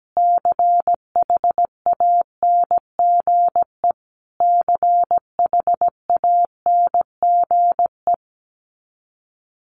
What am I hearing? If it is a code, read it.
Morse code, heard as a keyed tone, about 17 wpm, 700 Hz. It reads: CHANGE CHANGE
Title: Top 500 Words Repeated 17wpm